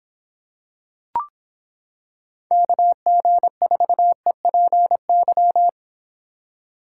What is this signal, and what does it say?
Morse code, heard as a keyed tone, about 26 wpm, 700 Hz. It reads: KG4EPY